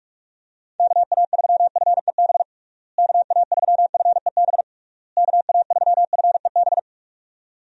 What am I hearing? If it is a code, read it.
Morse code, heard as a keyed tone, about 41 wpm, 700 Hz. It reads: KA3FEB KA3FEB KA3FEB
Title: USA Call Signs - 6 Characters - Repeated 3x - 45wpm